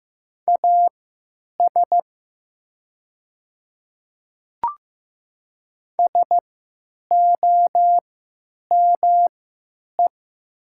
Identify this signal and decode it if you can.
Morse code, heard as a keyed tone, about 15 wpm, spaced as for 9 wpm, 700 Hz. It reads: AS SOME
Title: 3x Character Spacing - Top 100 Words - 15wpm